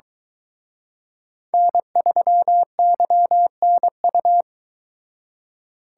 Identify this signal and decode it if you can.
Morse code, heard as a keyed tone, about 23 wpm, 700 Hz. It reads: N3YNU